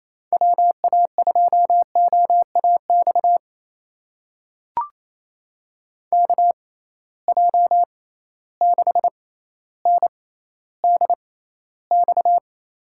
Morse code, heard as a keyed tone, about 27 wpm, spaced as for 14 wpm, 700 Hz. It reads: WA2OAX KJ6NDX
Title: USA Call Signs - 6 Characters - 6x Character Spacing - 28wpm